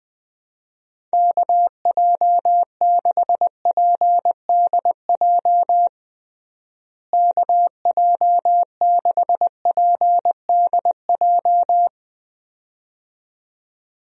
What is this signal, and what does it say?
Morse code, heard as a keyed tone, about 20 wpm, 700 Hz. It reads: KJ6PDJ KJ6PDJ
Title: USA Call Signs - 6 Characters - Repeated 2x - 20wpm